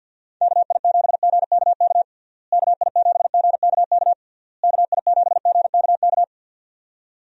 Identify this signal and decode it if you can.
Morse code, heard as a keyed tone, about 51 wpm, 700 Hz. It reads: KI7GKK KI7GKK KI7GKK
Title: USA Call Signs - 6 Characters - Repeated 3x - 50wpm